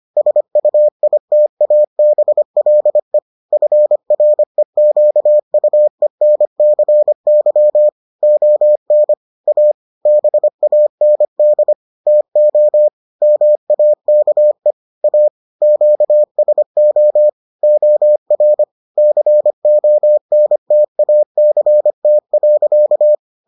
Morse code, heard as a keyed tone, about 25 wpm, 600 Hz. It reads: SUITABLE FREQUENCY ON A BAND TO MAKE A QSO OR CONTACT.